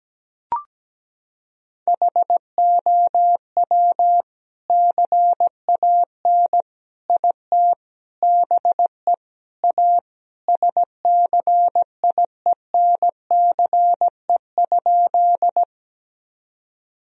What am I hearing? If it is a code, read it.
Morse code, heard as a keyed tone, about 17 wpm, 700 Hz. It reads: HOW CAN IT BE A SCIENCE?
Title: Sentences from Top 1000 Words - Review Part 2 - 17wpm